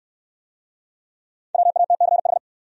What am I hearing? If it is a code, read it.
Morse code, heard as a keyed tone, about 60 wpm, 700 Hz. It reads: WATCH